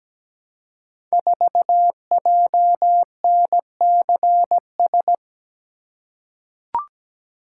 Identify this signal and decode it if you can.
Morse code, heard as a keyed tone, about 17 wpm, 700 Hz. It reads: 4JNCS
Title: Code Groups - 5 Letter:Numbers - 17wpm